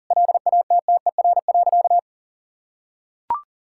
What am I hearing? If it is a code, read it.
Morse code, heard as a keyed tone, about 37 wpm, 700 Hz. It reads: LATTER.
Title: Sentences from Top 1600 Words 40wpm